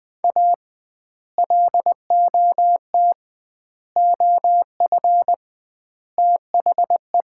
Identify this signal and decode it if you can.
Morse code, heard as a keyed tone, about 21 wpm, 700 Hz. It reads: A LOT OF THE
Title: 2x Word-Spacing - Sets of 4 Words - 20wpm